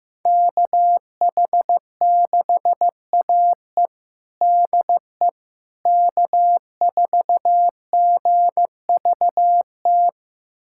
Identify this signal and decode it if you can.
Morse code, heard as a keyed tone, about 15 wpm, 700 Hz. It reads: KH6AE DE K4GVT
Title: USA Callsign DE Callsign - 5 Characters - 15wpm